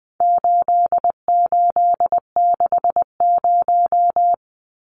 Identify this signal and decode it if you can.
Morse code, heard as a keyed tone, about 20 wpm, 700 Hz. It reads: 8860
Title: Numbers - 4-Digit 20wpm